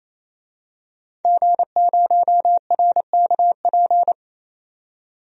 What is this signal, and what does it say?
Morse code, heard as a keyed tone, about 29 wpm, 690 Hz. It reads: G0RKP
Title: DXCC-100 Call Signs - 5 Characters - 28wpm